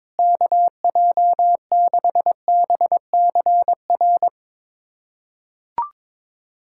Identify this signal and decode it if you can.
Morse code, heard as a keyed tone, about 22 wpm, 700 Hz. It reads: KJ6BCR
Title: USA Callsign DE Callsign - 6 Characters - 22wpm